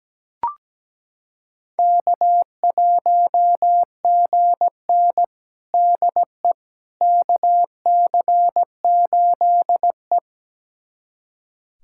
Morse code, heard as a keyed tone, about 17 wpm, 700 Hz. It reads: K1GN DE KC8E
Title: USA Callsign DE Callsign - 4 Characters - 17wpm